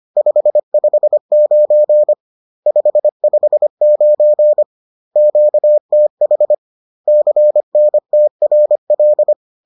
Morse code, heard as a keyed tone, about 25 wpm, 600 Hz. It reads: 559 559 QTH CNTRL